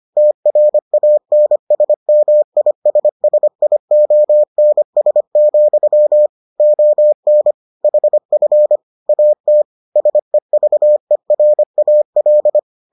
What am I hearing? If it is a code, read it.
Morse code, heard as a keyed tone, about 25 wpm, 600 Hz. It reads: TRANSMISSIONS, ON HF AT SEVERAL